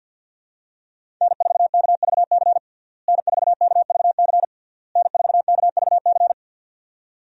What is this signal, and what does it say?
Morse code, heard as a keyed tone, about 52 wpm, 700 Hz. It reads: N4KVC N4KVC N4KVC
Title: USA Call Signs - 5 Characters - Repeated 3x - 50wpm